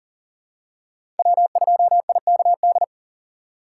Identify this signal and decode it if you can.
Morse code, heard as a keyed tone, about 40 wpm, 700 Hz. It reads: W2IKD
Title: USA Call Signs - 5 Characters - 2x Character Spacing - 40wpm